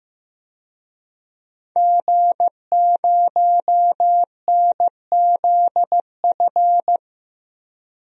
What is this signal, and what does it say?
Morse code, heard as a keyed tone, about 16 wpm, 700 Hz. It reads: G0NZF
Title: DXCC-100 Call Signs - 5 Characters - 15wpm